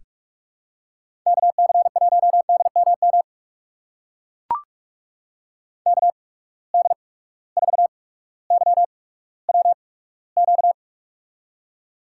Morse code, heard as a keyed tone, about 43 wpm, spaced as for 19 wpm, 700 Hz. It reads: KC1DMM KD4YWQ
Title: USA Call Signs - 6 Characters - 8x Character Spacing - 45wpm